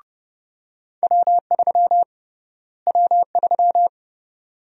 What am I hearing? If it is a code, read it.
Morse code, heard as a keyed tone, about 30 wpm, 700 Hz. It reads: W3 W3
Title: USA Call Sign Prefixes - Repeated 2x - 30wpm